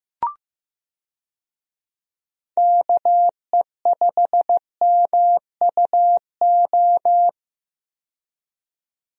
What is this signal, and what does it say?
Morse code, heard as a keyed tone, about 15 wpm, 700 Hz. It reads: KE5MUO
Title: USA Call Signs - 6 Characters - 15wpm